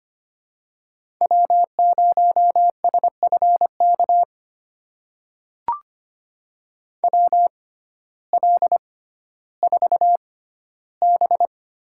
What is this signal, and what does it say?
Morse code, heard as a keyed tone, about 25 wpm, spaced as for 14 wpm, 700 Hz. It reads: W0SFK WL4B
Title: USA Call Signs - 6x Character Spacing - 25wpm